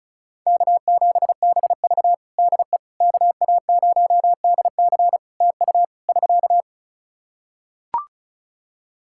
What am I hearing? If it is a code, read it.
Morse code, heard as a keyed tone, about 34 wpm, 700 Hz. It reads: K7BV DE KA0DC TU <SK>
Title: Practice QSO - Exchange 4 - Ending - 35wpm